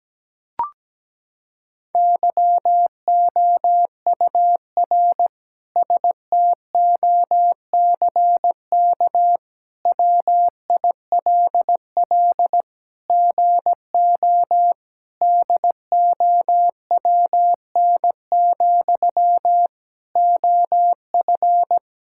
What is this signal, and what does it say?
Morse code, heard as a keyed tone, about 17 wpm, 700 Hz. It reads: YOUR STOCK WILL GO DOWN, OF
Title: Sentences from Top 1000 Words - Review Part 5 - 17wpm